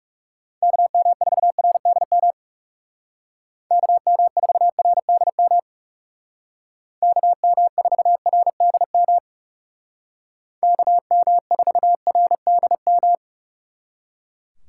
Morse code, heard as a keyed tone, about 39 wpm, 700 Hz. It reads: KM4RDM KM4RDM KM4RDM KM4RDM